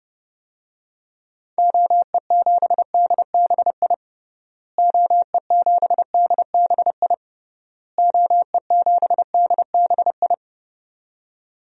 Morse code, heard as a keyed tone, about 32 wpm, 700 Hz. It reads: OE7DBI OE7DBI OE7DBI
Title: DXCC-100 Call Signs - 6 Characters - Repeated 3x - 30wpm